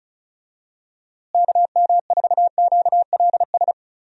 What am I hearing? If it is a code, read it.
Morse code, heard as a keyed tone, about 36 wpm, 700 Hz. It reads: KM4QLS
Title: USA Call Signs - 6 Characters - Repeated 2x - 35wpm